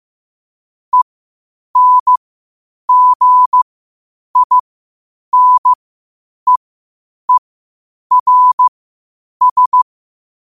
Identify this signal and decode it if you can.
Morse code, heard as a keyed tone, about 15 wpm, spaced as for 8 wpm, 1000 Hz. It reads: ENGINEERS
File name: qotdfiles / 8wpm